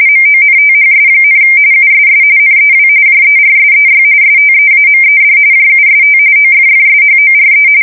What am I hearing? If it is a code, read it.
RTTY, about 45.45 baud. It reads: MONTHS, USS NEW JERSEY STEAMED JUST OFFSHORE
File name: fsk170 / battleshipnj